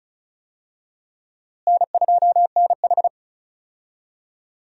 Morse code, heard as a keyed tone, about 34 wpm, 700 Hz. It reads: N2NH
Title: USA Call Signs - 4 Characters - Repeated 2x - 35wpm